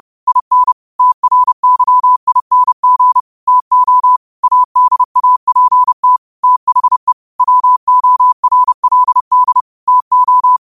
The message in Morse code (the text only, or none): IN TRYING TO ADAPT THE WORLD TO